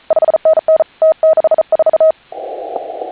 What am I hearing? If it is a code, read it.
Morse code, heard as a keyed tone, about 37 wpm, 630 Hz. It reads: 5NN T64
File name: HA6A CW